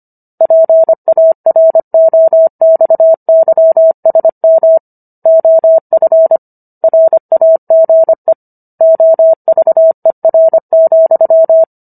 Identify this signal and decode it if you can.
Morse code, heard as a keyed tone, about 25 wpm, 650 Hz. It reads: PAROXYSM OF RAGE OVER,